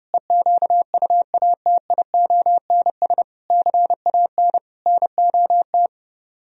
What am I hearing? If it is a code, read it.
Morse code, heard as a keyed tone, about 31 wpm, 700 Hz. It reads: EQUATIONS CAN NOT